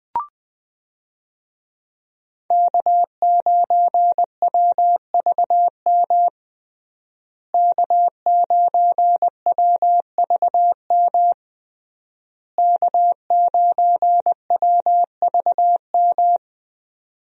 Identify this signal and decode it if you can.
Morse code, heard as a keyed tone, about 21 wpm, 700 Hz. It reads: K9WVM K9WVM K9WVM
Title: USA Call Signs - 5 Characters - Repeated 3x - 20wpm